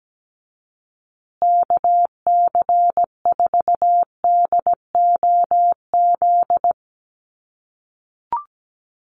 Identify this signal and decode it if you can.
Morse code, heard as a keyed tone, about 17 wpm, 700 Hz. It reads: KC4DOZ
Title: USA Call Signs Repeated 3x 17wpm